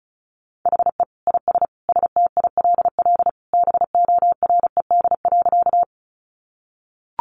Morse code, heard as a keyed tone, about 34 wpm, 700 Hz. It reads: HE IS STILL BORED.
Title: Easy Phrases 35wpm